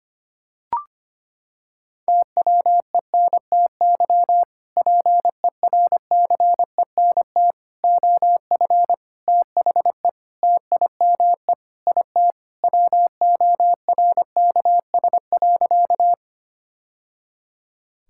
Morse code, heard as a keyed tone, about 25 wpm, 700 Hz. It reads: TWENTY PERCENT OF THE TIME IT WORKS.
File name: practice / 25wpm / Sentences with 2-Word Phrases - 25wpm